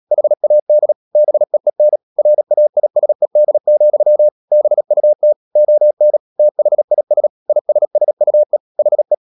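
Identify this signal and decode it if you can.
Morse code, heard as a keyed tone, about 37 wpm, 600 Hz. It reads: HAD BEEN RAISED, BUT ON THIS ISSUE HE
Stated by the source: Morse code practice at 35–39 wpm sending English text of Ken Follett's novel Never, from CW med Gnister.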